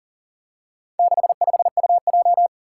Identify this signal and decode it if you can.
Morse code, heard as a keyed tone, about 40 wpm, 700 Hz. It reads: 65UJ
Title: Four Challenging Letter-Number Combinations 40wpm